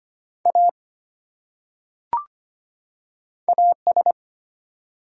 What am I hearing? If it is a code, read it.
Morse code, heard as a keyed tone, about 25 wpm, 700 Hz. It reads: A AS
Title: Top 100 Words 25wpm